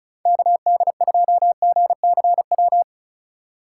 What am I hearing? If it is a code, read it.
Morse code, heard as a keyed tone, about 36 wpm, 700 Hz. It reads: KD2GCW
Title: USA Call Signs - 6 Characters - 3x Character Spacing - 35wpm